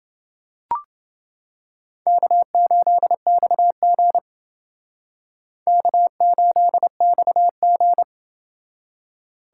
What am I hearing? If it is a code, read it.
Morse code, heard as a keyed tone, about 29 wpm, 700 Hz. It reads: K8XG K8XG